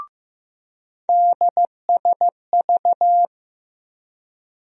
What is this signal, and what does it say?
Morse code, heard as a keyed tone, about 15 wpm, 700 Hz. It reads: DSV
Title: USA Call Sign Postfixes - 3 Letters 15wpm